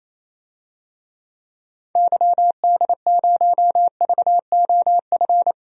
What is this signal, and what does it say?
Morse code, heard as a keyed tone, about 27 wpm, 700 Hz. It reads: YD0VOF
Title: DXCC-100 Call Signs - 6 Characters - 28wpm